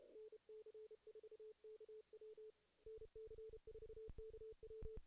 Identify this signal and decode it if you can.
Morse code, heard as a keyed tone, about 32 wpm, 440 Hz. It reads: NC4KW NC4KW
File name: 2016 JAN CW NAQP / 2016 Jan naqp k5ot - 0-8